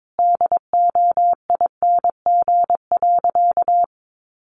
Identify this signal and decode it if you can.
Morse code, heard as a keyed tone, about 22 wpm, 700 Hz. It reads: DOING.
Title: Sentences from Top 1400 Words 22wpm